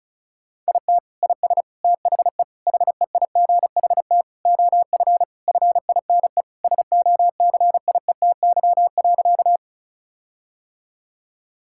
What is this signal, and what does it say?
Morse code, heard as a keyed tone, about 36 wpm, 700 Hz. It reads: IT IS THE HEIGHT OF FINE SOCIETY.
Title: Sentences from Top 1600 Words 35wpm